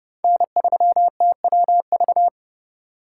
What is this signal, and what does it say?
Morse code, heard as a keyed tone, about 30 wpm, 700 Hz. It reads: N3TWV
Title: USA Call Signs - 5 Characters - 2x Character Spacing - 30wpm